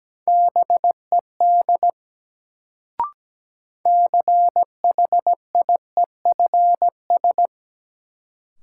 Morse code, heard as a keyed tone, about 17 wpm, 700 Hz. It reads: BED CHIEFS
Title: Sets of 5 Words Encore - New Word Drill - 17wpm